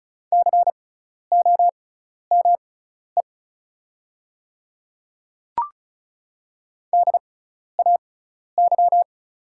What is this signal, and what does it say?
Morse code, heard as a keyed tone, about 36 wpm, spaced as for 12 wpm, 700 Hz. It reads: COME DAY
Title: 6x Character Spacing - Top 300 Words - 35wpm